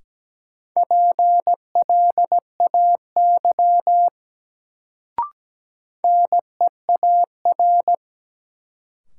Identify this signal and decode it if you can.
Morse code, heard as a keyed tone, about 17 wpm, 700 Hz. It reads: PLAY NEAR
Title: Sentences from Top 200 Words - New Word Drill - 17wpm